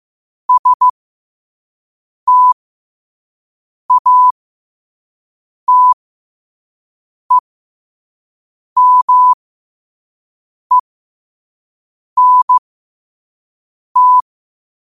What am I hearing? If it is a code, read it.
Morse code, heard as a keyed tone, about 15 wpm, spaced as for 5 wpm, 1000 Hz. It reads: STATEMENT
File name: 5wpm